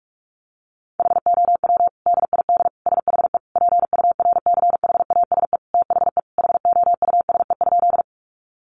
Morse code, heard as a keyed tone, about 44 wpm, 700 Hz. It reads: HOW DID SHE PURCHASE THE HOUSE?
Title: Sentences from Top 800 Words 45wpm